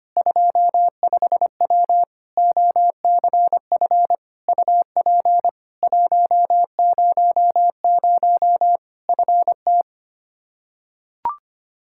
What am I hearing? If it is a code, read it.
Morse code, heard as a keyed tone, about 25 wpm, 700 Hz. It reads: 25W OCF UP 100 FT
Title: Practice QSO - Exchange 2 - Info - 25wpm